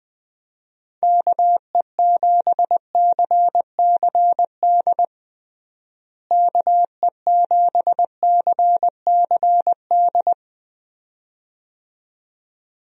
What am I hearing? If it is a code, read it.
Morse code, heard as a keyed tone, about 20 wpm, 700 Hz. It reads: KE7CCD KE7CCD
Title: USA Call Signs - 6 Characters - Repeated 2x - 20wpm